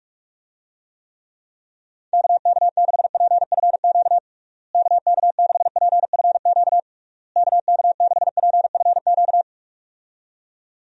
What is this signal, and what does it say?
Morse code, heard as a keyed tone, about 41 wpm, 700 Hz. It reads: KK6PFQ KK6PFQ KK6PFQ
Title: USA Call Signs - 6 Characters - Repeated 3x - 45wpm